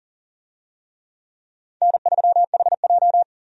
Morse code, heard as a keyed tone, about 40 wpm, 700 Hz. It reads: N3HJ
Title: USA Call Signs - 4 Characters - 40wpm